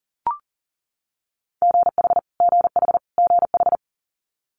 Morse code, heard as a keyed tone, about 40 wpm, 700 Hz. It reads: GH GH GH
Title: Multi-letter Phonemes Repeated 3x 40wpm